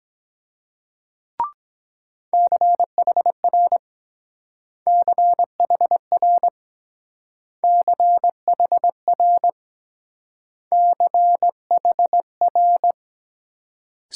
Morse code, heard as a keyed tone, about 22 wpm, 700 Hz. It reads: CHR CHR CHR CHR